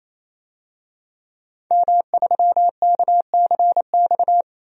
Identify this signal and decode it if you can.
Morse code, heard as a keyed tone, about 28 wpm, 700 Hz. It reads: M3KCX
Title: DXCC-100 Call Signs - 5 Characters - 28wpm